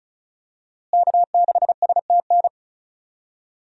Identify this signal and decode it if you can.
Morse code, heard as a keyed tone, about 34 wpm, 700 Hz. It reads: K6STN
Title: USA Call Signs - 5 Characters - Repeated 2x - 35wpm